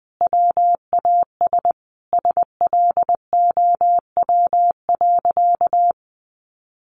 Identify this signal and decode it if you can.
Morse code, heard as a keyed tone, about 20 wpm, 700 Hz. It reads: WAS SLOW.
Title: Sentences from Top 1400 Words 20wpm